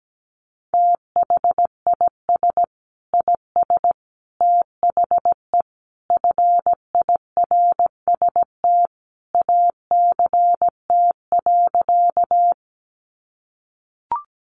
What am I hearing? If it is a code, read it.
Morse code, heard as a keyed tone, about 17 wpm, 700 Hz. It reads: THIS IS THE FIRST ACT.
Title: Sentences from Top 200 Words 17wpm